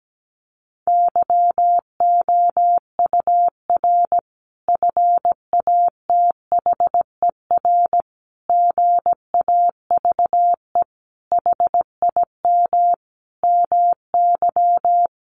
Morse code, heard as a keyed tone, about 17 wpm, 700 Hz. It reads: YOUR FATHER GAVE HIM MY
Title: Sentences from Top 1400 Words 17wpm